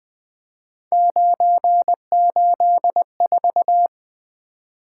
Morse code, heard as a keyed tone, about 20 wpm, 700 Hz. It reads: 984
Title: Three Letter-Number Combinations - 20wpm